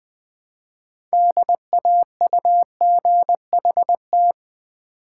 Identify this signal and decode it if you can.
Morse code, heard as a keyed tone, about 20 wpm, 700 Hz. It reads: DAUGHT
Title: Build A Word - 3 Letters At A Time - 8-Letter Words - 20wpm